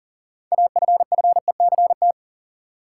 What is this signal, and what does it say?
Morse code, heard as a keyed tone, about 37 wpm, 700 Hz. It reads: AFFECT
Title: Six Letter Words - 40wpm